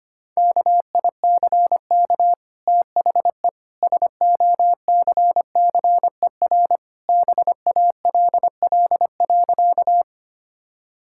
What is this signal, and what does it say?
Morse code, heard as a keyed tone, about 25 wpm, 700 Hz. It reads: KICK THE SOCCER BALL.
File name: Easy Sentences with Hints - 25wpm